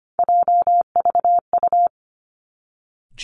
Morse code, heard as a keyed tone, about 25 wpm, 700 Hz. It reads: JVU
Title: Three Challenging Letter-Number Combinations 25wpm